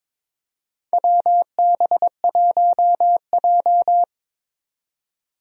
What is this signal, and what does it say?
Morse code, heard as a keyed tone, about 22 wpm, 700 Hz. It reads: WB1J
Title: USA Call Signs - 4 Characters - 22wpm